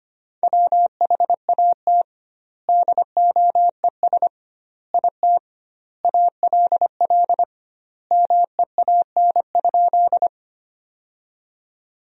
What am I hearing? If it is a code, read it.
Morse code, heard as a keyed tone, about 25 wpm, 700 Hz. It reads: WHAT DOES IT ALL MEAN?
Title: 2x Word-Spacing - Sentences from Top 200 Words - 25wpm